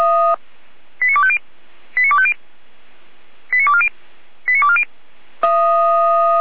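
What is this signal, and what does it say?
Five-tone selective call (ZVEI), heard as quick runs of five tones, tones of about 70 ms. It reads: 89250 89250 89240 89240
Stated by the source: ZVEI five-tone call received on SDR.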